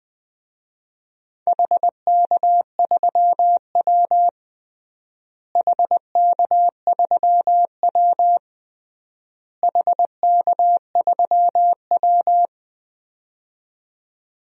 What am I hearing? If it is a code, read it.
Morse code, heard as a keyed tone, about 20 wpm, 700 Hz. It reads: HK3W HK3W HK3W
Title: DXCC-100 Call Signs - 4 Characters - Repeated 3x - 20wpm